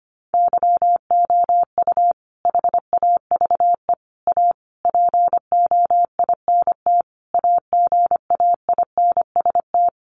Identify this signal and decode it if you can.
Morse code, heard as a keyed tone, about 25 wpm, 700 Hz. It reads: YOU HAVE A POINT AGAINST
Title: Sentences from Top 1400 Words 25wpm